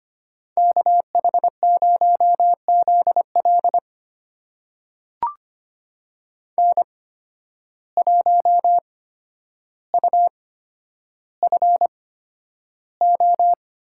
Morse code, heard as a keyed tone, about 25 wpm, spaced as for 11 wpm, 700 Hz. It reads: KH0ZL N1UFO
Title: USA Call Signs - 8x Character Spacing - 25wpm